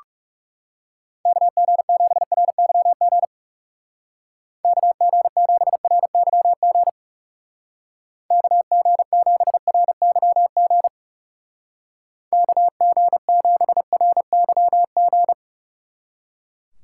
Morse code, heard as a keyed tone, about 35 wpm, 700 Hz. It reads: KG7RYG KG7RYG KG7RYG KG7RYG
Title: Speed Racing - USA Call Signs 30wpm